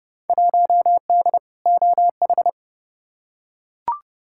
Morse code, heard as a keyed tone, about 30 wpm, 700 Hz. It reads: 1D OH
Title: Field Day Exchanges - 30wpm